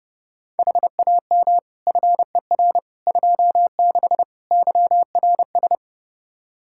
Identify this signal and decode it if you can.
Morse code, heard as a keyed tone, about 30 wpm, 700 Hz. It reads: HAM FER 26 YRS
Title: Practice QSO - Exchange 3 - More Info - 30wpm